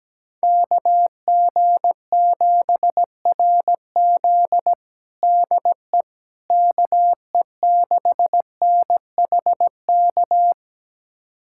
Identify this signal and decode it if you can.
Morse code, heard as a keyed tone, about 17 wpm, 700 Hz. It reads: KG7RZ DE KE6NHK
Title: USA Callsign DE Callsign 17wpm